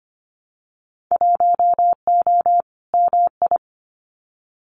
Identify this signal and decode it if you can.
Morse code, heard as a keyed tone, about 25 wpm, 700 Hz. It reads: 1O MI
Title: Winter Field Day Exchanges Encore 25wpm